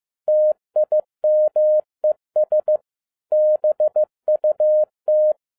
Morse code, heard as a keyed tone, about 15 wpm, 610 Hz. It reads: TIMES BUT